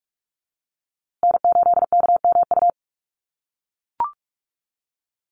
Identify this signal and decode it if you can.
Morse code, heard as a keyed tone, about 44 wpm, 700 Hz. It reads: N8KMU
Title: USA Call Signs Repeated 2x 45wpm